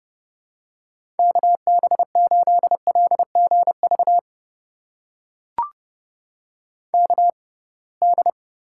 Morse code, heard as a keyed tone, about 30 wpm, spaced as for 23 wpm, 700 Hz. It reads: KB8LGV KD
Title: USA Call Signs - 6 Characters - 6x Character Spacing - 30wpm